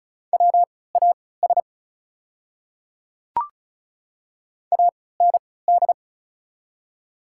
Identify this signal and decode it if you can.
Morse code, heard as a keyed tone, about 36 wpm, spaced as for 20 wpm, 700 Hz. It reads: WAS AND
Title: 3x Character Spacing - Top 100 Words - 35wpm